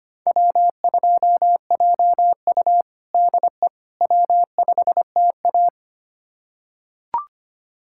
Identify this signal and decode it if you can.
Morse code, heard as a keyed tone, about 25 wpm, 700 Hz. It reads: W2JU DE W5TA
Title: USA Callsign DE Callsign - 4 Characters - 25wpm